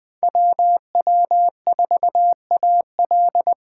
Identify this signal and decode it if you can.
Morse code, heard as a keyed tone, about 21 wpm, 700 Hz. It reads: WW4AL
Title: Speed Racing - USA 2x2 Call Signs 17wpm